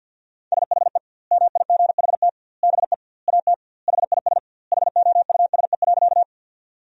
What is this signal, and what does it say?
Morse code, heard as a keyed tone, about 50 wpm, 700 Hz. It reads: SHE MIGHT BE AT HIS HOUSE.
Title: 2x Word-Spacing - Sentences from Top 300 Words - 50wpm